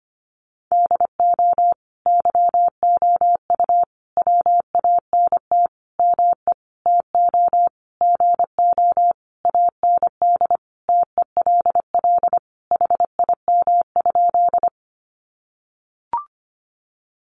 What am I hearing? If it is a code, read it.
Morse code, heard as a keyed tone, about 25 wpm, 700 Hz. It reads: DO YOU WANT ME TO GO AND TELL HIM?
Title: Sentences from Top 200 Words 25wpm